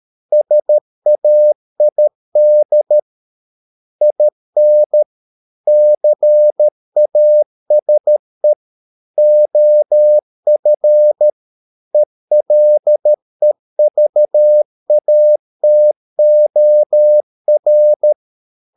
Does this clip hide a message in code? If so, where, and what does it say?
Yes: Morse code from 0.3 to 18.1 s, heard as a keyed tone, about 13 wpm, 600 Hz. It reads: SAID IN CASE OF ELEVATOR